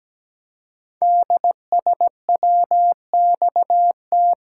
Morse code, heard as a keyed tone, about 17 wpm, 700 Hz. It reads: DSWXT
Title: Code Groups - 5 Letters - 17wpm